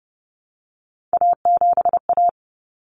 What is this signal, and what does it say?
Morse code, heard as a keyed tone, about 30 wpm, 700 Hz. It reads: A7A
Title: Common Cut Numbers - 3-Digit 30wpm